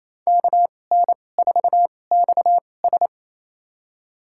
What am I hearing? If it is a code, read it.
Morse code, heard as a keyed tone, about 28 wpm, spaced as for 20 wpm, 700 Hz. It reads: KN4XS
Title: USA Call Signs - 2x Character Spacing - 28wpm